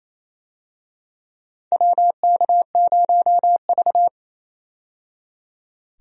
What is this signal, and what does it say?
Morse code, heard as a keyed tone, about 28 wpm, 700 Hz. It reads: WK0V
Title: USA Call Signs - 4 Characters - 28wpm